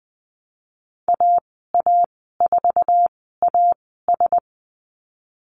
Morse code, heard as a keyed tone, about 20 wpm, spaced as for 15 wpm, 700 Hz. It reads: AA4AS
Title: 069 Lesson 12g - Letter d - Call Signs 20wpm